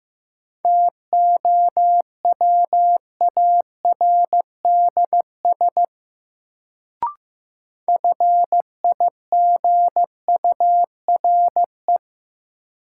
Morse code, heard as a keyed tone, about 15 wpm, 700 Hz. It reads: TOWARDS FIGURE
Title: Sentences from Top 1000 Words - Review Part 2 - New Word Drill - 15wpm